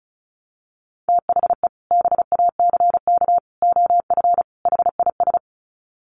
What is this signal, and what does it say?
Morse code, heard as a keyed tone, about 34 wpm, 700 Hz. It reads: THE BACK OF HIS
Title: Sets of 4 Words 35wpm